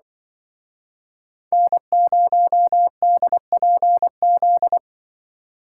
Morse code, heard as a keyed tone, about 24 wpm, 700 Hz. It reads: N0DPZ
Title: Speed Racing - USA 1x3 Call Signs 20wpm